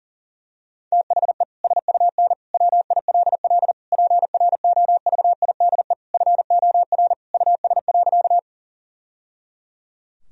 Morse code, heard as a keyed tone, about 40 wpm, 700 Hz. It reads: THE SUN WILL PROVIDE FOR US.
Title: Sentences from Top 1000 Words - Review Part 3 - 40wpm